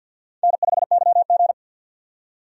Morse code, heard as a keyed tone, about 46 wpm, 700 Hz. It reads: N5YG
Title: USA Call Signs - 4 Characters - 50wpm